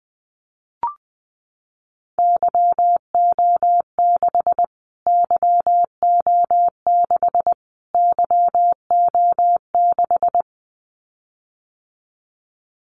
Morse code, heard as a keyed tone, about 20 wpm, 700 Hz. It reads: YO6 YO6 YO6
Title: Callsign Prefixes for Top 25 DXCC - Repeated 3x - 20wpm